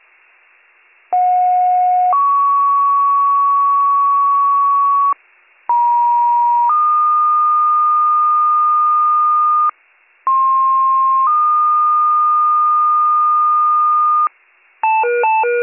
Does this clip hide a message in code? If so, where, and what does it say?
Yes: two-tone page from 1.1 to 14.3 s, heard as a one-second tone, then a three-second tone, about 730 then 1120 Hz, about 950 then 1220 Hz, about 1060 then 1220 Hz.